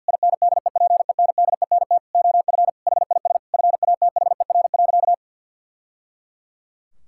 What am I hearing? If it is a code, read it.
Morse code, heard as a keyed tone, about 50 wpm, 700 Hz. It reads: INDEPENDENT OF HIS FATHER.
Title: Sentences from Top 1000 Words - Review Part 6 - 50wpm